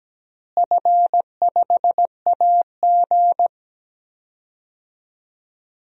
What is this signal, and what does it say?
Morse code, heard as a keyed tone, about 17 wpm, 700 Hz. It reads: F5AG
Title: DXCC-100 Call Signs - 4 Characters - Repeated 3x - 17wpm